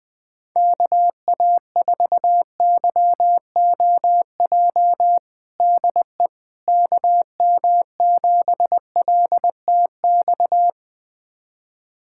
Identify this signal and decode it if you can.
Morse code, heard as a keyed tone, about 20 wpm, 700 Hz. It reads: KA4YOJ DE KM7LTX
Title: USA Callsign DE Callsign 20wpm